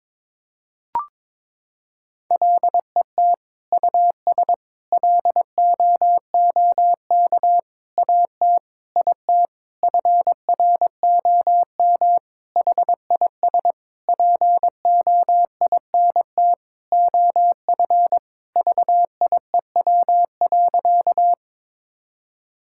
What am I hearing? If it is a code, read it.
Morse code, heard as a keyed tone, about 22 wpm, 700 Hz. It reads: LET US LOOK AT IT FROM HIS POINT OF VIEW.
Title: Sentences from Top 1000 Words - Review Part 6 - 22wpm